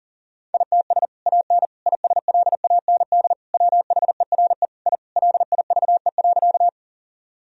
Morse code, heard as a keyed tone, about 40 wpm, 700 Hz. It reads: ITS AN ISLAND WHERE I LIVE.
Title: Sentences from Top 1000 Words - Review Part 3 - 40wpm